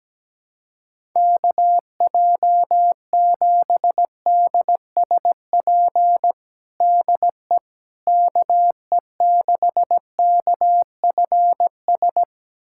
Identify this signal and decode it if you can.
Morse code, heard as a keyed tone, about 17 wpm, 700 Hz. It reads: KJ7DSP DE KE6KFS
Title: USA Callsign DE Callsign - 6 Characters - 17wpm